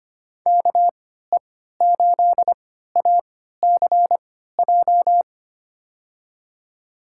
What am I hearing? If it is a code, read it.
Morse code, heard as a keyed tone, about 25 wpm, spaced as for 14 wpm, 700 Hz. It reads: KE8ACJ
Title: USA Call Signs - 6 Characters - 3x Character Spacing - 25wpm